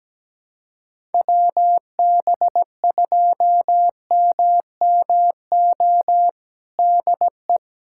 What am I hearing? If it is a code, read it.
Morse code, heard as a keyed tone, about 17 wpm, 700 Hz. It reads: WB2MMO DE
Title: USA Callsign DE Callsign - 6 Characters - 17wpm